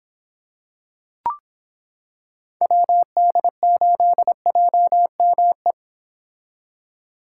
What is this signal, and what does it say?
Morse code, heard as a keyed tone, about 25 wpm, 700 Hz. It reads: WD8JME